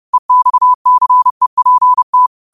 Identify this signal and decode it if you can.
Morse code, heard as a keyed tone, about 30 wpm, 1000 Hz. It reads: EXCEPT